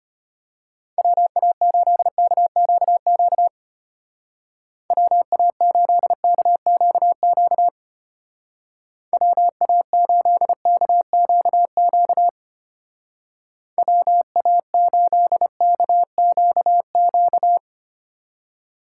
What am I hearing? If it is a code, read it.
Morse code, heard as a keyed tone, about 32 wpm, 700 Hz. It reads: WA8KQQ WA8KQQ WA8KQQ WA8KQQ